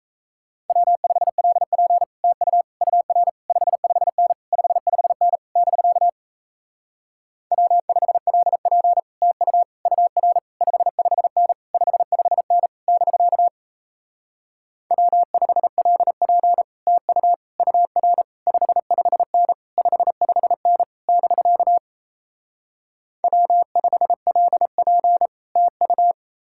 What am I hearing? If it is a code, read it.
Morse code, heard as a keyed tone, about 38 wpm, 700 Hz. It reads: W5LP TU UR 55N 55N <BK> W5LP TU UR 55N 55N <BK> W5LP TU UR 55N 55N <BK> W5LP TU